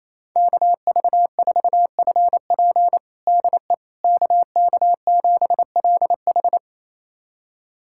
Morse code, heard as a keyed tone, about 27 wpm, 700 Hz. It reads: KV4FP DE KK7LH
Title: USA Callsign DE Callsign - 5 Characters - 28wpm